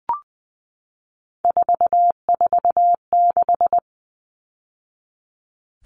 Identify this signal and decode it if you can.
Morse code, heard as a keyed tone, about 20 wpm, 700 Hz. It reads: 446
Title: Numbers - 3-Digit 20wpm